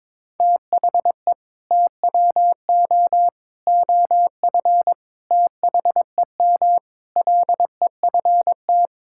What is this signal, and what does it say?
Morse code, heard as a keyed tone, about 22 wpm, 700 Hz. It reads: THE TWO OF THEM LEFT
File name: practice / 22wpm / Sentences with 4-Word Phrases - 22wpm